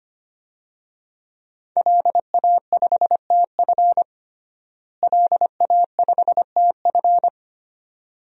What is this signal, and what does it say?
Morse code, heard as a keyed tone, about 25 wpm, 700 Hz. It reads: LA5TF LA5TF
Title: DXCC-100 Call Signs - 5 Characters - Repeated 2x - 25wpm